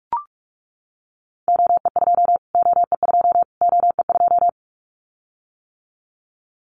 Morse code, heard as a keyed tone, about 44 wpm, 700 Hz. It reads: OE2 OE2 OE2
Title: Callsign Prefixes for Top 25 DXCC - Repeated 3x - 45wpm